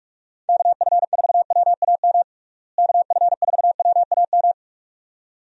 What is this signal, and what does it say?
Morse code, heard as a keyed tone, about 42 wpm, 700 Hz. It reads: KF4WAM KF4WAM
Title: USA Call Signs - 6 Characters - Repeated 2x - 45wpm